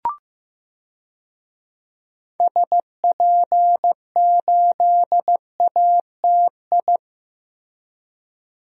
Morse code, heard as a keyed tone, about 15 wpm, 700 Hz. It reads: SP8ATI
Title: DXCC-100 Call Signs - 6 Characters - Repeated 3x - 15wpm